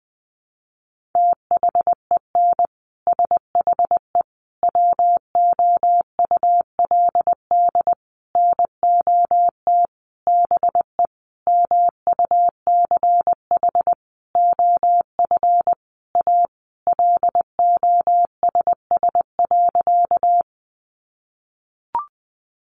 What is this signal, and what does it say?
Morse code, heard as a keyed tone, about 20 wpm, 700 Hz. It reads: THEN SHE WOULD NOT BE MUCH OF A LOSS.
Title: Sentences from Top 1400 Words 20wpm